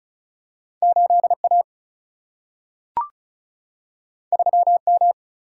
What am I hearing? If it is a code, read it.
Morse code, heard as a keyed tone, about 36 wpm, 700 Hz. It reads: 8A 3M
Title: Single-Letter Single-Number 35wpm